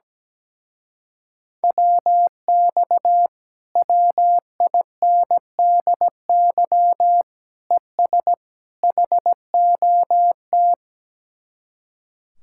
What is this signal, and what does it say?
Morse code, heard as a keyed tone, about 17 wpm, 700 Hz. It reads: WX WINDY ES HOT
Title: Speed-Racing Common QSO Phrases Encore - 17wpm